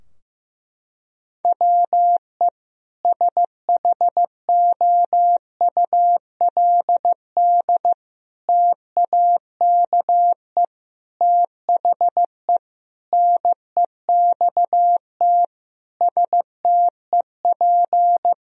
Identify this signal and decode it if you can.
Morse code, heard as a keyed tone, about 15 wpm, 700 Hz. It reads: WE SHOULD TAKE THE NEXT STEP
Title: Sentences from Top 1000 Words - Review Part 2 - 15wpm